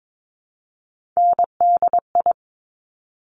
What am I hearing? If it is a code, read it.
Morse code, heard as a keyed tone, about 22 wpm, 700 Hz. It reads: NDI
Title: Top Three Letter Combinations in Words 22wpm